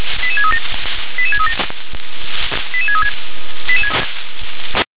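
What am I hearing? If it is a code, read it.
Five-tone selective call (ZVEI), heard as quick runs of five tones, tones of about 70 ms. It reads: 88637 88637 88636
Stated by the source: Received with an SDR as ZVEI five-tone selective call, code 88730.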